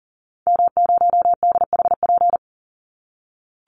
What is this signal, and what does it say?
Morse code, heard as a keyed tone, about 40 wpm, 700 Hz. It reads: M0DHP
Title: Call Signs - 40wpm